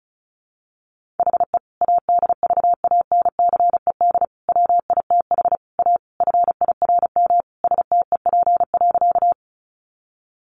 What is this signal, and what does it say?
Morse code, heard as a keyed tone, about 34 wpm, 700 Hz. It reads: HE ADVANCED WITH A FIRM STEP.
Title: Sentences from Top 1400 Words 35wpm